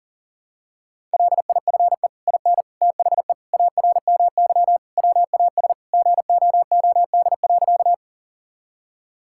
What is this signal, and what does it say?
Morse code, heard as a keyed tone, about 37 wpm, 700 Hz. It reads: LIFE IN THE ARMY WAS GOOD.
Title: Sentences from Top 1000 Words - Review Part 5 - 40wpm